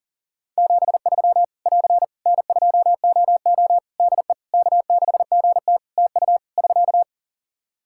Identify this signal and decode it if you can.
Morse code, heard as a keyed tone, about 40 wpm, 700 Hz. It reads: 73 <AR> N2OO DE K6GT TU <SK>
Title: Practice QSO - Exchange 4 - Ending - 40wpm